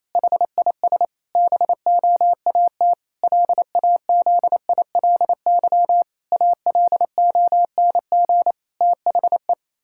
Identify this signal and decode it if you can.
Morse code, heard as a keyed tone, about 28 wpm, 700 Hz. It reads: HIS BOAT LAZILY ALONG THE
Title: Pangram Challenge v2 - 28wpm